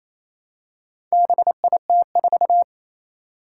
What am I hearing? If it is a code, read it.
Morse code, heard as a keyed tone, about 28 wpm, 700 Hz. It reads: BIT4
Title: Four Challenging Letter-Number Combinations 28wpm